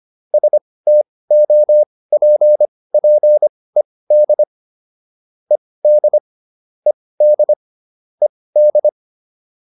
Morse code, heard as a keyed tone, about 25 wpm, spaced as for 18 wpm, 600 Hz. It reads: STOPPED ED ED ED
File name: ED7 18